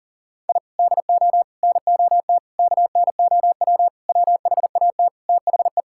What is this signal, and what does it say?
Morse code, heard as a keyed tone, about 40 wpm, 700 Hz. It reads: I DO NOT KNOW WHAT THE
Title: Sentences from Top 1000 Words - Review Part 6 - 40wpm